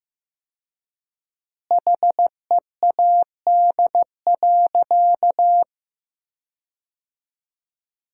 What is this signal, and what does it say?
Morse code, heard as a keyed tone, about 15 wpm, 700 Hz. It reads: HEAD.
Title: 4x Word-Spacing - Sentences from Top 300 Words - 15wpm